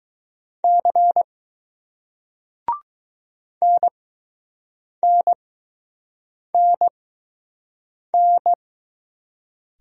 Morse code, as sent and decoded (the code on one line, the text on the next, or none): -.-. / -. -. -. -.
C NNNN